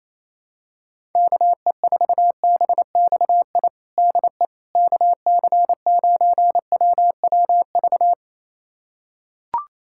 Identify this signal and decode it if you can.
Morse code, heard as a keyed tone, about 28 wpm, 700 Hz. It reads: KE4BXI DE KC9WWV
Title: USA Callsign DE Callsign 28wpm